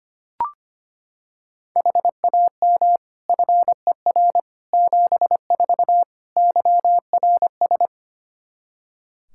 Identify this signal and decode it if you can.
Morse code, heard as a keyed tone, about 25 wpm, 700 Hz. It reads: HAM FER 74 YRS
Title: Practice QSO - Exchange 3 - More Info - 25wpm